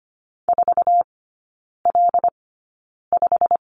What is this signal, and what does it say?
Morse code, heard as a keyed tone, about 25 wpm, spaced as for 9 wpm, 700 Hz. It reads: 4L5
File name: Lesson 15e - Number 5 - Three Characters 25wpm